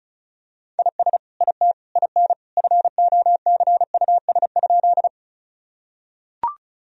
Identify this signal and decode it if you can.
Morse code, heard as a keyed tone, about 36 wpm, 700 Hz. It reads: IS IT IN FOCUS?
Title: Sentences from Top 1000 Words - Review Part 4 - 35wpm